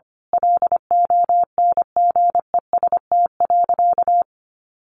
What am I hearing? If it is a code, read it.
Morse code, heard as a keyed tone, about 25 wpm, 700 Hz. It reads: LONGEST.
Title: A Sound Of Thunder 25wpm